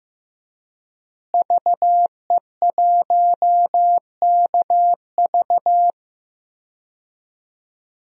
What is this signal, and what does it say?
Morse code, heard as a keyed tone, about 15 wpm, 700 Hz. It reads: VE1KV
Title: DXCC-100 Call Signs - 5 Characters - Repeated 2x - 15wpm